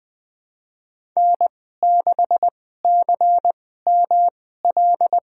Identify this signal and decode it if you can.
Morse code, heard as a keyed tone, about 20 wpm, spaced as for 15 wpm, 700 Hz. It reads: N6CML
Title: USA Call Signs - 5 Characters - 2x Character Spacing - 20wpm